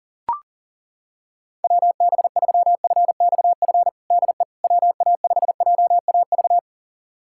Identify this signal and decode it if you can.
Morse code, heard as a keyed tone, about 40 wpm, 700 Hz. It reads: WB3FXF DE WA5JAV
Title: USA Callsign DE Callsign 40wpm